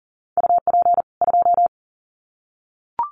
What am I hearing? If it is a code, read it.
Morse code, heard as a keyed tone, about 40 wpm, 700 Hz. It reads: UP 2
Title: Common QSO Phrases 40wpm